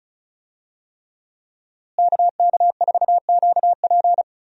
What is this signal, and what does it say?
Morse code, heard as a keyed tone, about 36 wpm, 700 Hz. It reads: KK4QP
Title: USA Call Signs - 5 Characters - Repeated 3x - 35wpm